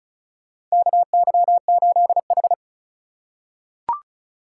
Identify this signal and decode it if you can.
Morse code, heard as a keyed tone, about 36 wpm, 700 Hz. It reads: KY8H
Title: USA Call Signs - 4 Characters - 3x Character Spacing - 35wpm